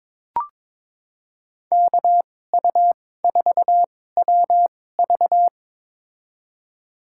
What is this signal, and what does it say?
Morse code, heard as a keyed tone, about 22 wpm, spaced as for 16 wpm, 700 Hz. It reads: KU4WV
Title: USA Call Signs - 5 Characters - 2x Character Spacing - 22wpm